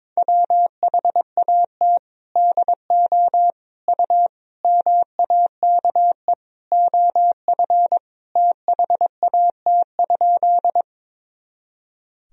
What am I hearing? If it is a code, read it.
Morse code, heard as a keyed tone, about 22 wpm, 700 Hz. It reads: WHAT DO U MAKE OF THAT?
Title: Sentences from Top 100 Words and Abbreviations 22wpm